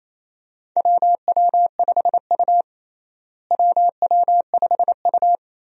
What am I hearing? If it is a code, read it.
Morse code, heard as a keyed tone, about 28 wpm, 700 Hz. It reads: WW5U WW5U
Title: USA Call Signs - 4 Characters - Repeated 2x - 28wpm